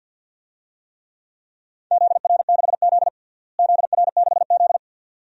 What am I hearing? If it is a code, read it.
Morse code, heard as a keyed tone, about 52 wpm, 700 Hz. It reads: ZR6Z ZR6Z
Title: DXCC-100 Call Signs - 4 Characters - Repeated 2x - 50wpm